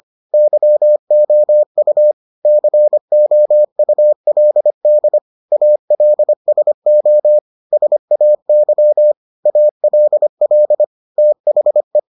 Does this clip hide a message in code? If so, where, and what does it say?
Yes: Morse code from 0.3 to 12.0 s, heard as a keyed tone, about 25 wpm, 600 Hz. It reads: YOU COULD ALSO SAY ALL THE